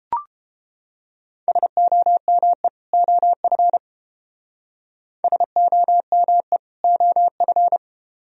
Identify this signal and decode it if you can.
Morse code, heard as a keyed tone, about 32 wpm, 700 Hz. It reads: SOME OF SOME OF